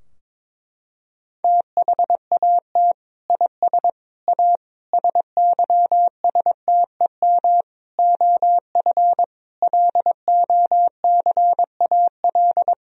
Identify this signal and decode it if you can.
Morse code, heard as a keyed tone, about 22 wpm, 700 Hz. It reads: THAT IS A SYSTEM OF LOCAL
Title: Sentences from Top 1000 Words - Review Part 3 - 22wpm